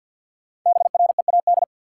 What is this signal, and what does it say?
Morse code, heard as a keyed tone, about 53 wpm, 700 Hz. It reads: BREAD
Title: Five Letter Words - 50wpm